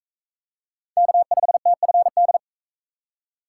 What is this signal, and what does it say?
Morse code, heard as a keyed tone, about 43 wpm, 700 Hz. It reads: K5TFD